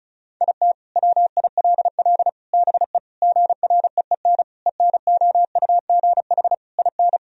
Morse code, heard as a keyed tone, about 36 wpm, 700 Hz. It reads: IT WILL BE GREEN ENOUGH IN
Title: Sentences from Top 1000 Words - Review Part 2 - 35wpm